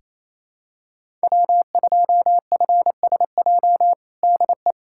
Morse code, heard as a keyed tone, about 28 wpm, 700 Hz. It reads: W2FSJ DE
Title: USA Callsign DE Callsign - 5 Characters - 28wpm